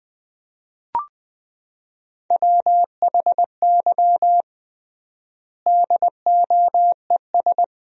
Morse code, heard as a keyed tone, about 21 wpm, 700 Hz. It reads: WHY DOES
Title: 3x Word-Spacing - Sentences from Top 500 Words - 20wpm